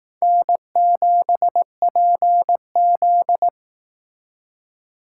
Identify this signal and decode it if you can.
Morse code, heard as a keyed tone, about 18 wpm, 700 Hz. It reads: N7PZ
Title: Speed Racing - USA 1x2 Call Signs 15wpm